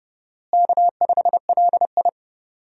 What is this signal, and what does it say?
Morse code, heard as a keyed tone, about 30 wpm, 700 Hz. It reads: K5LI
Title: USA Call Signs - 4 Characters - Repeated 2x - 30wpm